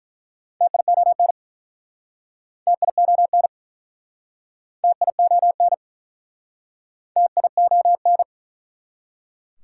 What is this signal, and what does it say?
Morse code, heard as a keyed tone, about 48 wpm, 700 Hz. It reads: TION TION TION TION